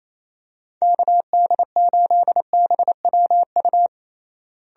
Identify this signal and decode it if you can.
Morse code, heard as a keyed tone, about 28 wpm, 700 Hz. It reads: KD8BWU
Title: USA Call Signs - 6 Characters - Repeated 2x - 28wpm